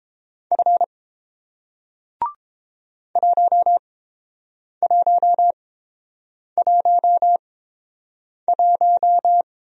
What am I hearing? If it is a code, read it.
Morse code, heard as a keyed tone, about 31 wpm, spaced as for 7 wpm, 700 Hz. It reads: F 1111